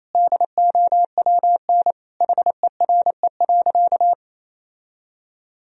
Morse code, heard as a keyed tone, about 28 wpm, 700 Hz. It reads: DOWN HERE.
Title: Sentences from Top 1000 Words - Review Part 2 - 28wpm